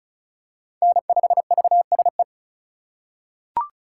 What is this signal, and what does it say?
Morse code, heard as a keyed tone, about 36 wpm, 700 Hz. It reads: N5VSE
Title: USA Call Signs - 5 Characters - 2x Character Spacing - 35wpm